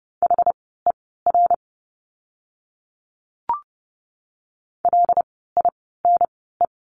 Morse code, heard as a keyed tone, about 30 wpm, spaced as for 17 wpm, 700 Hz. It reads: HER LINE
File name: course / 30:10wpm / 115 Lesson 19f - Letter w - Words 30wpm